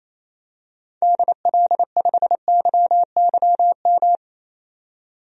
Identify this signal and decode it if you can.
Morse code, heard as a keyed tone, about 28 wpm, 700 Hz. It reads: DL5YYM
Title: DXCC-100 Call Signs - 6 Characters - 28wpm